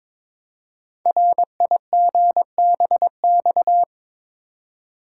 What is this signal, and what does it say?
Morse code, heard as a keyed tone, about 23 wpm, 700 Hz. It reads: RIGBX
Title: Code Groups - 5 Letters - 22wpm